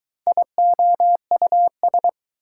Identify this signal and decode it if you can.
Morse code, heard as a keyed tone, about 20 wpm, 700 Hz. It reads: IOUS